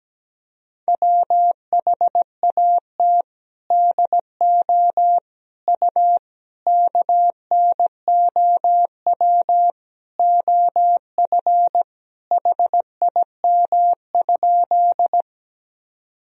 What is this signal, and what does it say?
Morse code, heard as a keyed tone, about 17 wpm, 700 Hz. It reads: WHAT DO U KNOW OF HIM?
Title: Sentences from Top 100 Words and Abbreviations 17wpm